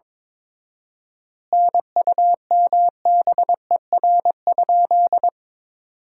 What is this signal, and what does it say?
Morse code, heard as a keyed tone, about 22 wpm, 700 Hz. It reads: NUMBER?
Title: 4x Word-Spacing - Sentences from Top 100 Words - 22wpm